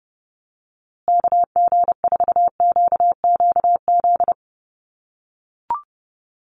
Morse code, heard as a keyed tone, about 30 wpm, 700 Hz. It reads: KG4QQZ
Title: USA Call Signs Repeated 2x 30wpm